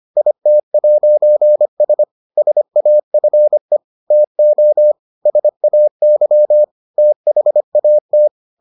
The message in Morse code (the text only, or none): IT'S SAFE TO SAY THAT